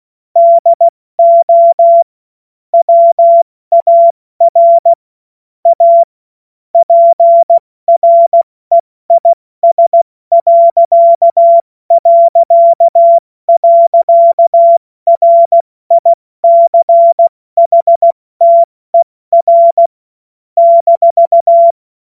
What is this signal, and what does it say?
Morse code, heard as a keyed tone, about 16 wpm, 680 Hz. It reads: DO WAR A PREIS...RICHTER -
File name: Witz-12-August-2025-70-F80